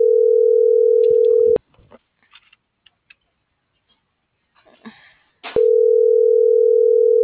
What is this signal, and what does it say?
Ringback tone (North America), 440 + 480 Hz.